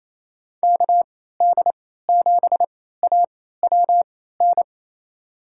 Morse code, heard as a keyed tone, about 28 wpm, spaced as for 16 wpm, 700 Hz. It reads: KD7AWN
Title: USA Call Signs - 6 Characters - 3x Character Spacing - 28wpm